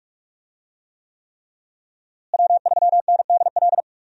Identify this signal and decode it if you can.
Morse code, heard as a keyed tone, about 45 wpm, 700 Hz. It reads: W3NDL
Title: USA Call Signs - 5 Characters - Repeated 3x - 45wpm